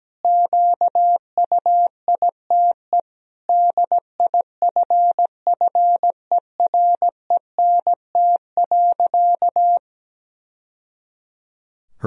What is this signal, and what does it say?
Morse code, heard as a keyed tone, about 17 wpm, 700 Hz. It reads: QUITE DIFFERENT.
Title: Sentences from Top 1000 Words - Review Part 3 - 17wpm